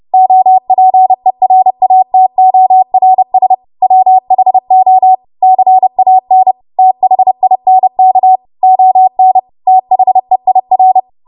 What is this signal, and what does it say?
Morse code, heard as a keyed tone, about 30 wpm, 750 Hz. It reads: OPERATORS WHO CAN THINK ON THEIR
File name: Morse / 30WPM